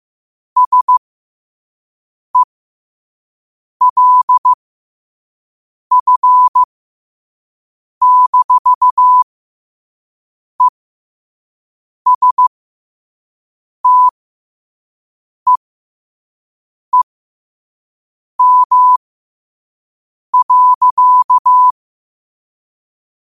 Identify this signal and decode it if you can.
Morse code, heard as a keyed tone, about 15 wpm, spaced as for 5 wpm, 1000 Hz. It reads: SELF-ESTEEM.
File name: qotdfiles / 5wpm